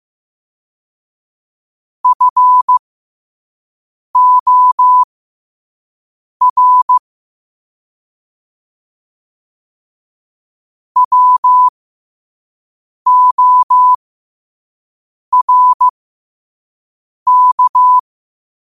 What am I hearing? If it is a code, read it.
Morse code, heard as a keyed tone, about 15 wpm, spaced as for 5 wpm, 1000 Hz. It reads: FOR WORK
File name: qotdfiles / 5wpm